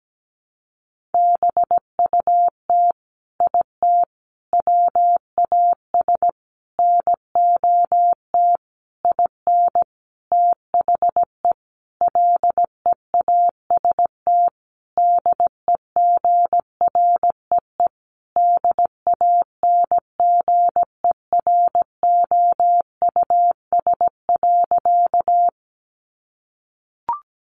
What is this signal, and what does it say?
Morse code, heard as a keyed tone, about 17 wpm, 700 Hz. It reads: BUT IT WAS NOT IN THE LEAST DEGREE DANGEROUS.
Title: Sentences from Top 1500 Words 17wpm